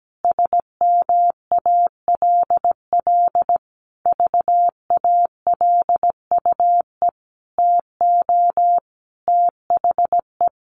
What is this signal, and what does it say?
Morse code, heard as a keyed tone, about 17 wpm, 700 Hz. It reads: SMALL VALUE TO THE
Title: Sentences from Top 1500 Words 17wpm